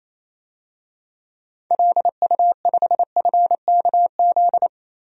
Morse code, heard as a keyed tone, about 28 wpm, 700 Hz. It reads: LU5FKZ
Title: DXCC-100 Call Signs - 6 Characters - 28wpm